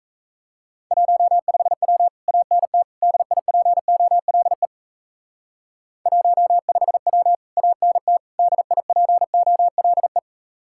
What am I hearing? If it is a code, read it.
Morse code, heard as a keyed tone, about 36 wpm, 700 Hz. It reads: 15W ANT DIPOLE 15W ANT DIPOLE